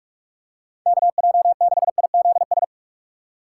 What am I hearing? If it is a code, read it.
Morse code, heard as a keyed tone, about 46 wpm, 700 Hz. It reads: KJ6IZS
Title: USA Call Signs - 6 Characters - 6x Character Spacing - 45wpm